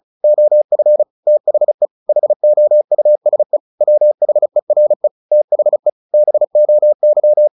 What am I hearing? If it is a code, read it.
Morse code, heard as a keyed tone, about 34 wpm, 600 Hz. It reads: OF THE HOUSE WHERE THE BOY
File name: the old man and the sea / the old man and the sea 1 35wpm